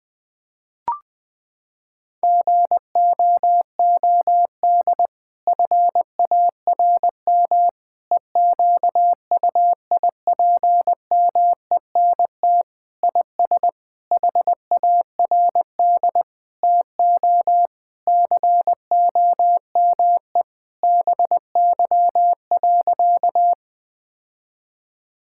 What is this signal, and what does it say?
Morse code, heard as a keyed tone, about 20 wpm, 700 Hz. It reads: GOOD FARM EQUIPMENT IS HARD TO COME BY.
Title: Sentences from Top 1000 Words - Review Part 5 - 20wpm